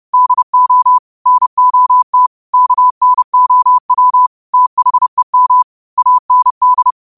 Morse code, heard as a keyed tone, about 30 wpm, 1000 Hz. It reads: DO NOT KNOW THEM AND
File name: qotdfiles / 30wpm